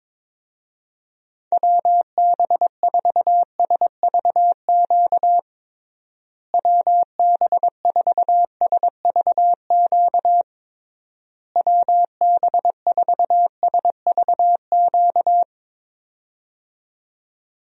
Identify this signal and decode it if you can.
Morse code, heard as a keyed tone, about 22 wpm, 700 Hz. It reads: WB4SVQ WB4SVQ WB4SVQ
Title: USA Call Signs - 6 Characters - Repeated 3x - 22wpm